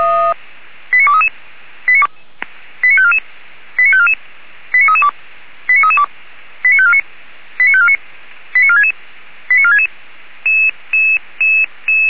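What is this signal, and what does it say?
Five-tone selective call (ZVEI), heard as quick runs of five tones, tones of about 70 ms. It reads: 89233 89544 89544 89332 89332 89549 89549 89466 89466
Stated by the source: Five-tone ZVEI selective call, address 89384, received on an SDR.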